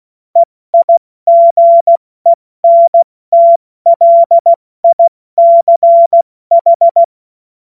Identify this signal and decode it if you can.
Morse code, heard as a keyed tone, about 16 wpm, 680 Hz. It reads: EIGENTLICH
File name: Morsekurs / Witze / Witz 10. Januar 2023 70 F80